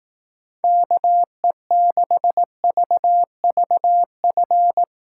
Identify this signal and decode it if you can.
Morse code, heard as a keyed tone, about 18 wpm, 700 Hz. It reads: KE6VVF
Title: Speed Racing - USA 2x3 Call Signs 15wpm